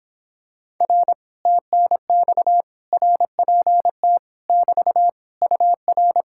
Morse code, heard as a keyed tone, about 26 wpm, 700 Hz. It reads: R TNX RPT <BT> UR